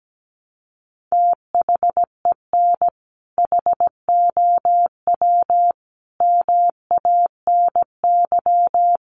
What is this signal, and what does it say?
Morse code, heard as a keyed tone, about 17 wpm, 700 Hz. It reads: THEN HOW MANY
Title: Sentences from Top 100 Words 17wpm